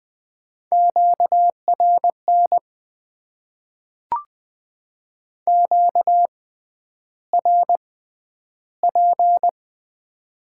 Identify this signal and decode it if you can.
Morse code, heard as a keyed tone, about 20 wpm, spaced as for 10 wpm, 700 Hz. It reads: QRN QRP
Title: Top 90 QSO Elements - 6x Character Spacing - 20wpm